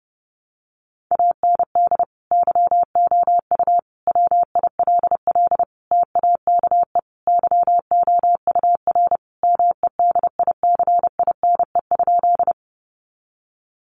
Morse code, heard as a keyed tone, about 30 wpm, 700 Hz. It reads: AND YOU WILL TAKE YOUR MEDICINE?
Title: Sentences from Top 900 Words 30wpm